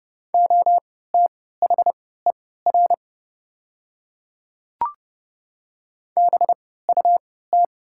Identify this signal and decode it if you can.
Morse code, heard as a keyed tone, about 30 wpm, spaced as for 17 wpm, 700 Hz. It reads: OTHER BUT
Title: 3x Character Spacing - Top 100 Words - 30wpm